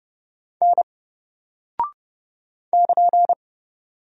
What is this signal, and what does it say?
Morse code, heard as a keyed tone, about 30 wpm, 700 Hz. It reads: N <KN>
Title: Single Character (Letter:Number:Punctuation:Prosign) 30wpm